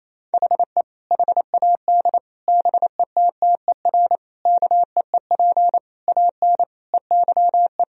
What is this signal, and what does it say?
Morse code, heard as a keyed tone, about 28 wpm, 700 Hz. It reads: HE HAD BETTER KEEP AN EYE
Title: Sentences from Top 1000 Words - Review Part 1 - 28wpm